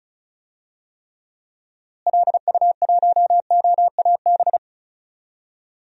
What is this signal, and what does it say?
Morse code, heard as a keyed tone, about 36 wpm, 700 Hz. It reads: LU1OAB
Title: DXCC-100 Call Signs - 6 Characters - Repeated 3x - 35wpm